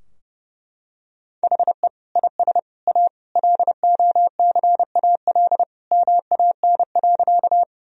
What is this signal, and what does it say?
Morse code, heard as a keyed tone, about 30 wpm, 700 Hz. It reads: HE IS A LOCAL MAN.
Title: Sentences from Top 1000 Words - Review Part 3 - 30wpm